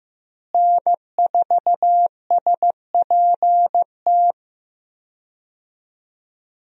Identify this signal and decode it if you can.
Morse code, heard as a keyed tone, about 15 wpm, 700 Hz. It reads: N4SPT
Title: USA Call Signs - 5 Characters - Repeated 2x - 15wpm